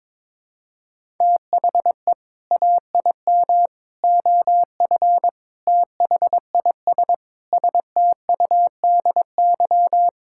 Morse code, heard as a keyed tone, about 22 wpm, 700 Hz. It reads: THE AIM OF THIS STUDY
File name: Sentences with 7-Word Phrases - 22wpm